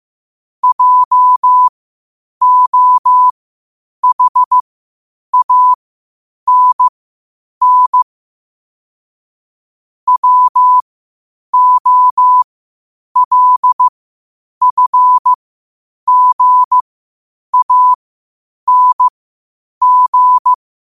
Morse code, heard as a keyed tone, about 15 wpm, spaced as for 8 wpm, 1000 Hz. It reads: JOHANN WOLFGANG
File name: qotdfiles / 8wpm